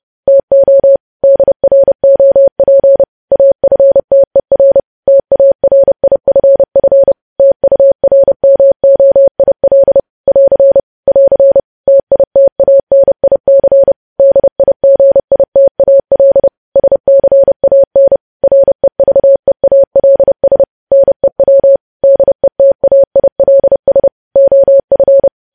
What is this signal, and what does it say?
Morse code, heard as a keyed tone, about 30 wpm, 550 Hz. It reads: TO DROP AFTER TARIFF TURMOIL <AR> <AR> TITANIC DIGITAL SCAN REVEALS NEW DETAILS OF
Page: news headlines in morse code at 30 wpm